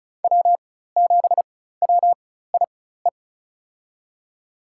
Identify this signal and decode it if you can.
Morse code, heard as a keyed tone, about 36 wpm, spaced as for 16 wpm, 700 Hz. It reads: W7WIE
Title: USA Call Signs - 4x Character Spacing - 35wpm